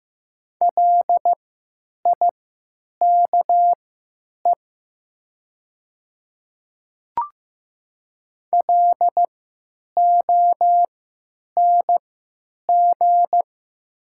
Morse code, heard as a keyed tone, about 15 wpm, spaced as for 9 wpm, 700 Hz. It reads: LIKE LONG
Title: 3x Character Spacing - Top 100 Words - 15wpm